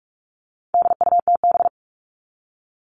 Morse code, heard as a keyed tone, about 44 wpm, 700 Hz. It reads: DUTB
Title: All Cut Numbers - 4-Digit 45wpm